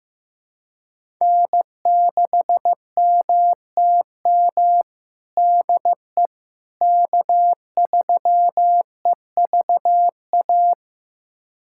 Morse code, heard as a keyed tone, about 15 wpm, 700 Hz. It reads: N6MTM DE K3EVA
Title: USA Callsign DE Callsign - 5 Characters - 15wpm